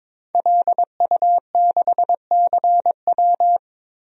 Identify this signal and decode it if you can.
Morse code, heard as a keyed tone, about 22 wpm, 700 Hz. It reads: LU6CW
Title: DXCC-100 Call Signs - 5 Characters - Repeated 2x - 22wpm